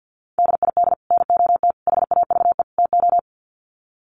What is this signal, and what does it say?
Morse code, heard as a keyed tone, about 50 wpm, 700 Hz. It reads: DID NOT HAVE TO
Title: Sets of 4 Words 50wpm